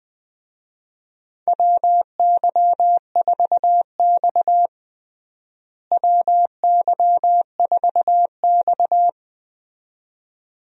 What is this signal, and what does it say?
Morse code, heard as a keyed tone, about 20 wpm, 700 Hz. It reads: WY4X WY4X
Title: USA Call Signs - 4 Characters - Repeated 2x - 20wpm